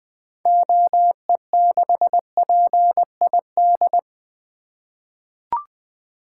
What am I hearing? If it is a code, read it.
Morse code, heard as a keyed tone, about 20 wpm, 700 Hz. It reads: OE6PID
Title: 2023 SOTA Activator Call Signs - 20wpm